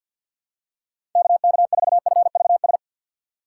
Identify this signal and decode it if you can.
Morse code, heard as a keyed tone, about 52 wpm, 700 Hz. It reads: KK4FVS
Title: USA Call Signs - 3x Character Spacing - 50wpm